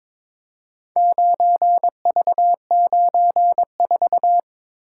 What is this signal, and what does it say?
Morse code, heard as a keyed tone, about 22 wpm, 700 Hz. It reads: 9V94
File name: practice / 22wpm / Four Letter-Number Combinations - 22wpm